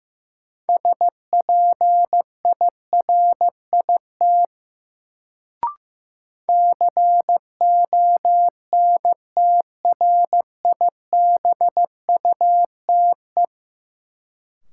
Morse code, heard as a keyed tone, about 15 wpm, 700 Hz. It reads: SPIRIT CONTRIBUTE
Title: Sentences from Top 1400 Words - New Word Drill Alt - 15wpm